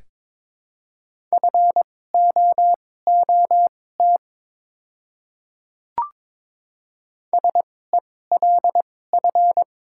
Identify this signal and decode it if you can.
Morse code, heard as a keyed tone, about 22 wpm, spaced as for 16 wpm, 700 Hz. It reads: FOOT SELF
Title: 2x Character Spacing - Top 500 Words - 22wpm